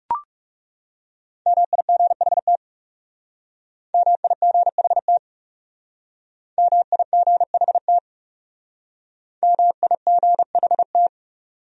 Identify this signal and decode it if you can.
Morse code, heard as a keyed tone, about 36 wpm, 700 Hz. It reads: MIGHT MIGHT MIGHT MIGHT
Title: Speed-Racing Top 300 Words 30wpm